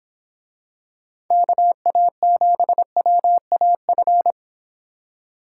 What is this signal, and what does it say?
Morse code, heard as a keyed tone, about 24 wpm, 700 Hz. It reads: KA7WAF